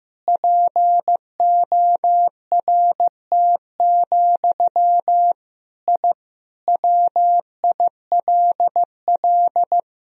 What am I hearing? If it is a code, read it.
Morse code, heard as a keyed tone, about 15 wpm, 700 Hz. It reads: PORT, I WILL
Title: Sentences from Top 1000 Words - Review Part 1 - 15wpm